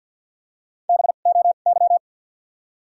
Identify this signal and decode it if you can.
Morse code, heard as a keyed tone, about 48 wpm, spaced as for 35 wpm, 700 Hz. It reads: BOY
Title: 2x Character Spacing - Top 300 Words - 50wpm